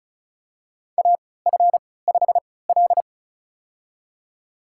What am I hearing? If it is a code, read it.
Morse code, heard as a keyed tone, about 34 wpm, spaced as for 20 wpm, 700 Hz. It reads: AF5L
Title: USA Call Signs - 3x Character Spacing - 35wpm